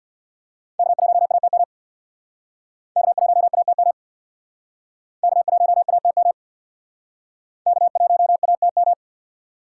Morse code, heard as a keyed tone, about 66 wpm, 700 Hz. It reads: K1ATM K1ATM K1ATM K1ATM